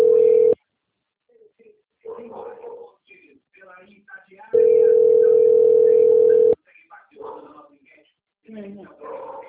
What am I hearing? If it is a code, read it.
Ringback tone (North America), 440 + 480 Hz, with speech.
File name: out- - - - - .32516